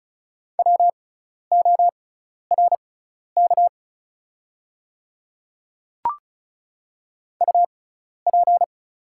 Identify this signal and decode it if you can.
Morse code, heard as a keyed tone, about 36 wpm, spaced as for 12 wpm, 700 Hz. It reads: WORK UP
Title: 6x Character Spacing - Top 200 Words - 35wpm